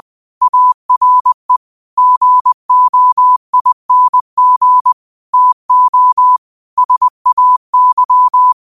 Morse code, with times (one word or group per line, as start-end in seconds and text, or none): 0.4-1.6 ARE
2.0-4.9 GOING
5.3-6.4 TO
6.8-8.5 SAY